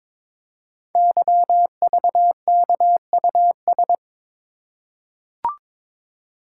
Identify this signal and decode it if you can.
Morse code, heard as a keyed tone, about 22 wpm, 700 Hz. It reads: YVKUS
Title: Code Groups - 5 Letters - 22wpm